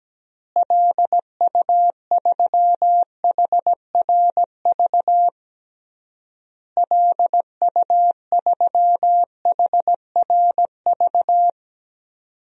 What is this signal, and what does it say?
Morse code, heard as a keyed tone, about 17 wpm, 700 Hz. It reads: LU3HRV LU3HRV
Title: DXCC-100 Call Signs - 6 Characters - Repeated 3x - 17wpm